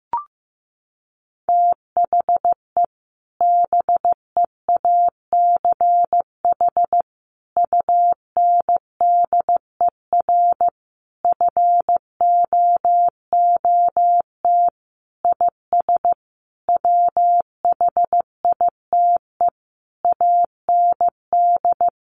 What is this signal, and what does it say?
Morse code, heard as a keyed tone, about 15 wpm, 700 Hz. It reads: THE BEACH UNDER FOOT IS WHITE AND
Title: Sentences from Top 1500 Words 15wpm